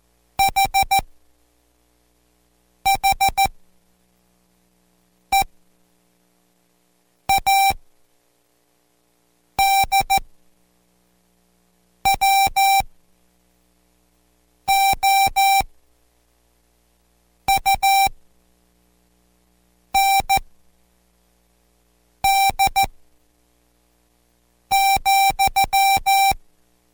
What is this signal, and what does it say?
Morse code, heard as a keyed tone, about 14 wpm, spaced as for 4 wpm, 780 Hz. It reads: H HEAD WOUND,